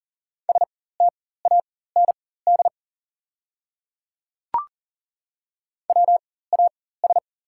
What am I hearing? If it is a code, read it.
Morse code, heard as a keyed tone, about 40 wpm, spaced as for 19 wpm, 690 Hz. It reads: STAND WAS
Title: 4x Character Spacing - Top 300 Words - 40wpm